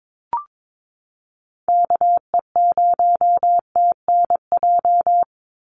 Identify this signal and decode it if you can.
Morse code, heard as a keyed tone, about 22 wpm, 700 Hz. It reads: KE0TNJ
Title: USA Call Signs 22wpm